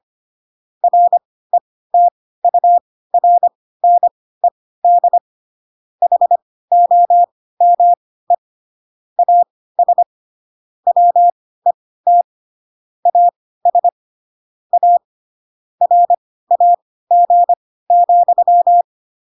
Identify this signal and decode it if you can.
Morse code, heard as a keyed tone, about 25 wpm, spaced as for 16 wpm, 700 Hz. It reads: RETURNED HOME AS WET AS A RAG,